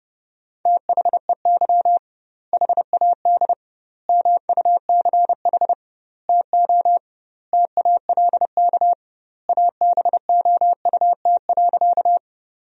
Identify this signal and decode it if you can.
Morse code, heard as a keyed tone, about 30 wpm, 700 Hz. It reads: THEY HAD MUCH TO TALK ABOUT.
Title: 2x Word-Spacing - Sentences from Top 400 Words - 30wpm